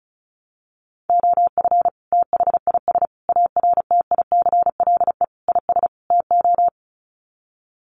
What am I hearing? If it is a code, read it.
Morse code, heard as a keyed tone, about 34 wpm, 700 Hz. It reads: OF THIS ARTICLE IS TO
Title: Sets of 5 Words Encore2 35wpm